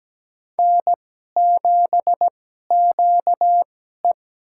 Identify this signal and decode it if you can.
Morse code, heard as a keyed tone, about 17 wpm, spaced as for 12 wpm, 700 Hz. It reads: N7QE
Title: USA Call Signs - 4 Characters - 2x Character Spacing - 17wpm